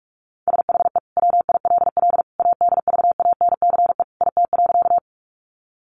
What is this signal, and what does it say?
Morse code, heard as a keyed tone, about 44 wpm, 700 Hz. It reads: SHE WILL ADVANCE IT.
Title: Sentences from Top 1400 Words 45wpm